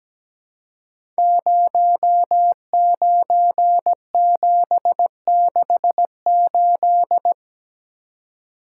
Morse code, heard as a keyed tone, about 17 wpm, 700 Hz. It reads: 09768
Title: Code Groups - 5 Letter:Numbers - 17wpm